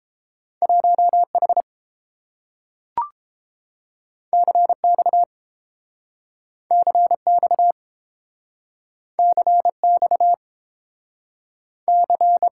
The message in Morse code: 1H CX CX CX C

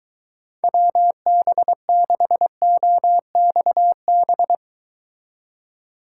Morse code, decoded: WB6OXB

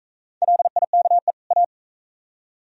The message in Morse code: LIKE A

